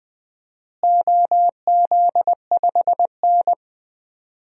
Morse code, heard as a keyed tone, about 20 wpm, 700 Hz. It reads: OZ5N
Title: DXCC-100 Call Signs - 4 Characters - Repeated 3x - 20wpm